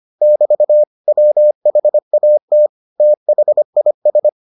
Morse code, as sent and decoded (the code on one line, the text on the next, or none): -...- / .-- .... .- - / - .... .. ...
<BT> WHAT THIS